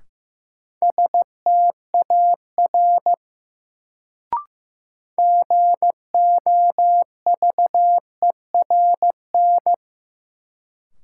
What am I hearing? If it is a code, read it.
Morse code, heard as a keyed tone, about 15 wpm, 700 Hz. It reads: STAR GOVERN
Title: Sentences from Top 1000 Words - Review Part 2 - New Word Drill - 15wpm